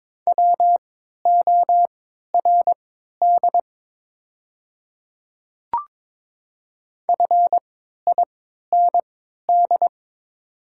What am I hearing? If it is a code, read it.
Morse code, heard as a keyed tone, about 22 wpm, spaced as for 12 wpm, 700 Hz. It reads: WORD FIND
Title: 3x Character Spacing - Top 100 Words - 22wpm